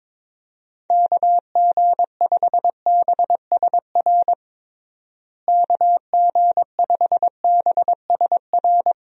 Morse code, heard as a keyed tone, about 22 wpm, 700 Hz. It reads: KG5BSR KG5BSR
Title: USA Call Signs - 6 Characters - Repeated 3x - 22wpm